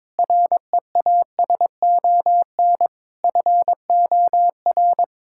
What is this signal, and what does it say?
Morse code, heard as a keyed tone, about 21 wpm, 700 Hz. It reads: REASON FOR
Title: Sentences from Top 1600 Words 22wpm